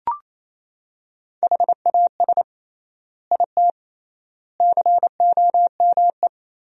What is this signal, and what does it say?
Morse code, heard as a keyed tone, about 28 wpm, 700 Hz. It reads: HAS IT COME
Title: 3x Word-Spacing - Sentences from Top 100 Words - 28wpm